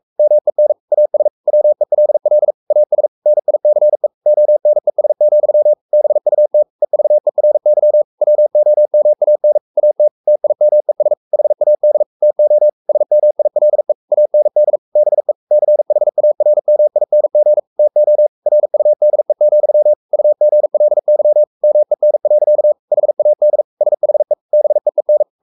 Morse code, heard as a keyed tone, about 44 wpm, 600 Hz. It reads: MEN AS WELL AS NICE ONES, BUT EVERY WOMAN AT TIMES HAD TO SMILE AND BE CHARMING TO RUDE, UGLY MEN. HAD SHE BEEN
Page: Romanen Never af Ken Follet på engelsk.